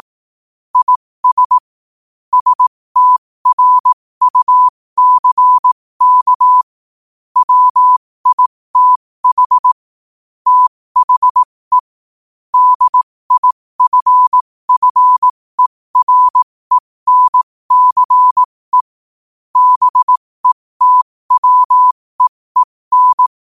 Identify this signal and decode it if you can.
Morse code, heard as a keyed tone, about 18 wpm, spaced as for 15 wpm, 1000 Hz. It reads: IS STRUCK WITH THE DIFFERENCE BETWEEN
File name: 15wpm